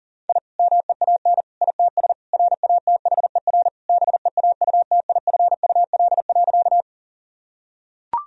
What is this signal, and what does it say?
Morse code, heard as a keyed tone, about 40 wpm, 700 Hz. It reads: I MEAN ITS RATHER BEAUTIFUL.
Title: Sentences from Top 1000 Words - Review Part 4 - 40wpm